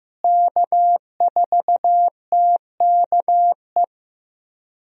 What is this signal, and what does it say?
Morse code, heard as a keyed tone, about 15 wpm, 700 Hz. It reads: K4TKE
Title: USA Call Signs - 5 Characters - Repeated 3x - 15wpm